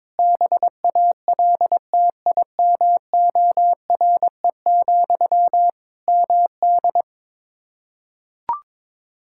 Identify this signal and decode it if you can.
Morse code, heard as a keyed tone, about 22 wpm, 700 Hz. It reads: BALTIMORE, MD
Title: Practice QSO - Exchange 1 - Beginning - 22wpm